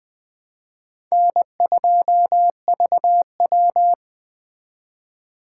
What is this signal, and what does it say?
Morse code, heard as a keyed tone, about 20 wpm, 700 Hz. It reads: N2VW
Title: Speed Racing - USA 1x2 Call Signs 17wpm